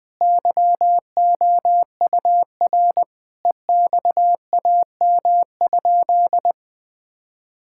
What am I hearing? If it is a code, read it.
Morse code, heard as a keyed tone, about 20 wpm, 700 Hz. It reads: YOUR EXAM?
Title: Sentences from Top 1000 Words - Review Part 5 - 20wpm